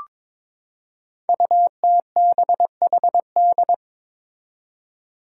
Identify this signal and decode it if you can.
Morse code, heard as a keyed tone, about 22 wpm, 700 Hz. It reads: UTBHD
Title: Five Challenging Letter-Number Combinations 22wpm